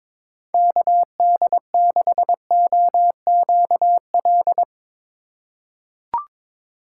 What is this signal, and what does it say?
Morse code, heard as a keyed tone, about 22 wpm, 700 Hz. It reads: KD6OQL
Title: USA Call Signs - 6 Characters - 3x Character Spacing - 22wpm